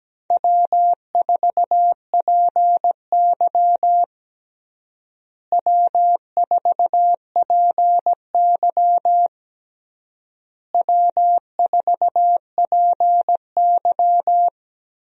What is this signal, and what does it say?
Morse code, heard as a keyed tone, about 17 wpm, 700 Hz. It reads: W4PY W4PY W4PY
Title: USA Call Signs - 4 Characters - Repeated 3x - 17wpm